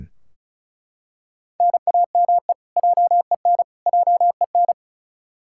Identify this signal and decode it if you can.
Morse code, heard as a keyed tone, about 34 wpm, 700 Hz. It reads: NAME JEN JEN
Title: K1USN SST QSO Practice 35wpm